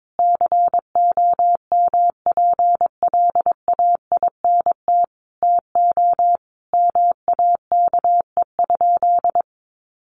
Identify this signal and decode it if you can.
Morse code, heard as a keyed tone, about 22 wpm, 700 Hz. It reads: COMPLAINT TO MAKE?
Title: Sentences from Top 1400 Words 22wpm